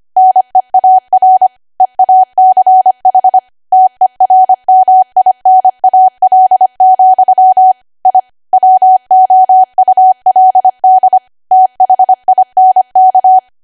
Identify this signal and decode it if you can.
Morse code, heard as a keyed tone, about 25 wpm, 750 Hz. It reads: NEAR EACH TERMINAL, I WOULD THINK
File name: Archive / 25 WPM / 25WPM